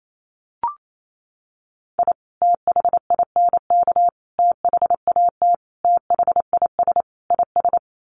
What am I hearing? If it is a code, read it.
Morse code, heard as a keyed tone, about 28 wpm, 700 Hz. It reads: I THINK THAT THIS IS
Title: Sets of 5 Words Encore2 28wpm